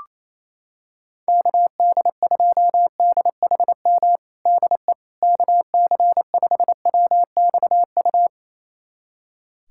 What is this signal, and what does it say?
Morse code, heard as a keyed tone, about 28 wpm, 700 Hz. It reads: KD2DHM DE KC5WXU
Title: USA Callsign DE Callsign - 6 Characters - 28wpm